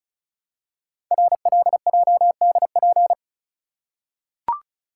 Morse code, heard as a keyed tone, about 36 wpm, 700 Hz. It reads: RLJDP
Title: Code Groups - 5 Letters - 35wpm